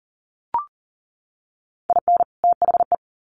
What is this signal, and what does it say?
Morse code, heard as a keyed tone, about 40 wpm, 700 Hz. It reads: IN THE
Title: Sets of 2 Words 40wpm